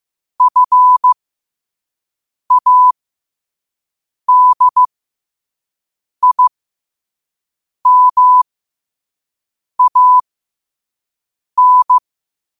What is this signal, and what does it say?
Morse code, heard as a keyed tone, about 15 wpm, spaced as for 5 wpm, 1000 Hz. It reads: FADIMAN
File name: 5wpm